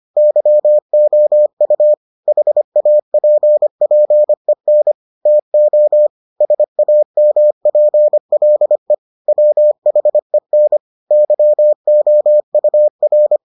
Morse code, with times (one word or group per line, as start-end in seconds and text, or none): 0.2-1.9 YOU
2.3-4.9 HAPPEN
5.3-6.1 TO
6.4-8.9 SAMPLE
9.3-10.8 WHEN
11.1-13.4 YOUR